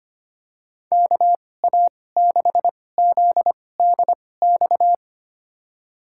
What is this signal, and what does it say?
Morse code, heard as a keyed tone, about 25 wpm, spaced as for 18 wpm, 700 Hz. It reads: KA6ZDX
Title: USA Call Signs - 2x Character Spacing - 25wpm